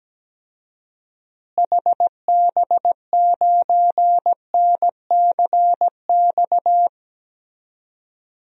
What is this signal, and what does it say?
Morse code, heard as a keyed tone, about 17 wpm, 700 Hz. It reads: HB9NCX
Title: DXCC-100 Call Signs - 6 Characters - Repeated 2x - 17wpm